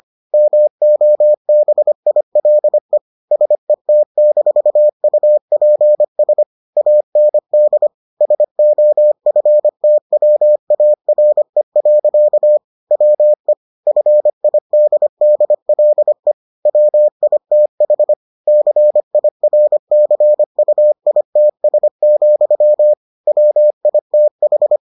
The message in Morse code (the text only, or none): MOBILE SET-UPS AND SOFTWARE. WE FIDDLE WITH CIRCUITS, WITH